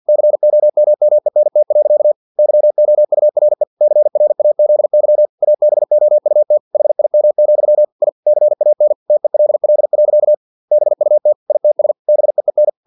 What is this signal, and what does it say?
Morse code, heard as a keyed tone, about 50 wpm, 600 Hz. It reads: COMMENT. YOURE CRAZY ABOUT HIM, I CAN TELL. BUT ITS BEEN
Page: Never 49wpm